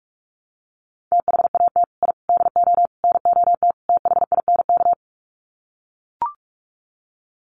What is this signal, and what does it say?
Morse code, heard as a keyed tone, about 44 wpm, 700 Hz. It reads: THAT I DO NOT THINK
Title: Sets of 5 Words Encore 45wpm